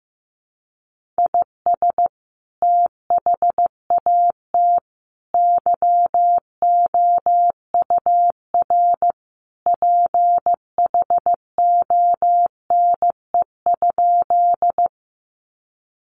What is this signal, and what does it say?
Morse code, heard as a keyed tone, about 15 wpm, 700 Hz. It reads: IS THAT YOUR PHONE?
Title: Sentences from Top 1100 Words 15wpm